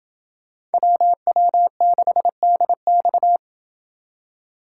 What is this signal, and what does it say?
Morse code, heard as a keyed tone, about 28 wpm, 700 Hz. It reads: WW6DX